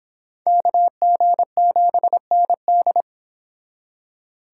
Morse code, heard as a keyed tone, about 26 wpm, 700 Hz. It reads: KG7ND